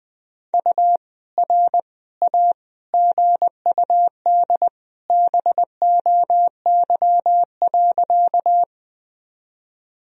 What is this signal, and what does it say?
Morse code, heard as a keyed tone, about 21 wpm, 700 Hz. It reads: U R A GUD BOY.
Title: Sentences from Top 200 Words and Abbreviations 20wpm